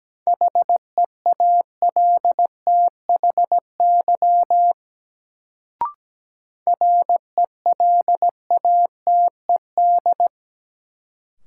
Morse code, heard as a keyed tone, about 17 wpm, 700 Hz. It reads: HEALTHY RELATED
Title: Sentences from Top 1000 Words - Review Part 5 - New Word Drill - 17wpm